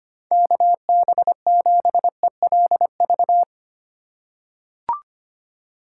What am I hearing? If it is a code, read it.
Morse code, heard as a keyed tone, about 25 wpm, 700 Hz. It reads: KB7ELV
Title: USA Call Signs - 6 Characters - 6x Character Spacing - 25wpm